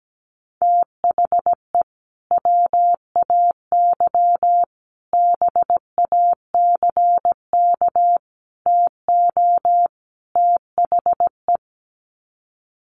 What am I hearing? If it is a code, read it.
Morse code, heard as a keyed tone, about 17 wpm, 700 Hz. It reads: THE WAY BACK TO THE
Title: Sets of 5 Words Encore2 17wpm